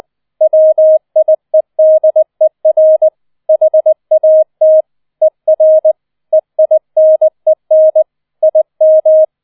Morse code, heard as a keyed tone, about 20 wpm, 630 Hz. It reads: WIEDER HAT ER EINEN IM
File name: 20wpm